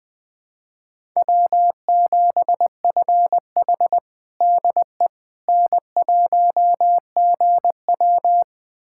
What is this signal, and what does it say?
Morse code, heard as a keyed tone, about 21 wpm, 700 Hz. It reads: W7FH DE N1GW
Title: USA Callsign DE Callsign - 4 Characters - 20wpm